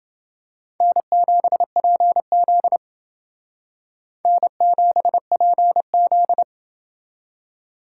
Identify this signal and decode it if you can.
Morse code, heard as a keyed tone, about 29 wpm, 700 Hz. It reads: N7PZ N7PZ